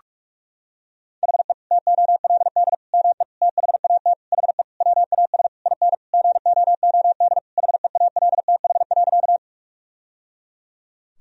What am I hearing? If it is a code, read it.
Morse code, heard as a keyed tone, about 41 wpm, 700 Hz. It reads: HE TOLD ME THAT HE WAS IN GOOD HEALTH.
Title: Sentences from Top 1000 Words - Review Part 3 - 45wpm